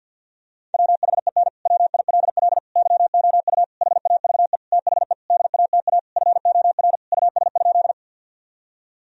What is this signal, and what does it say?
Morse code, heard as a keyed tone, about 50 wpm, 700 Hz. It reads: WHEN WILL YOU HAVE THE DATA FOR US?
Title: Sentences from Top 1000 Words - Review Part 3 - 50wpm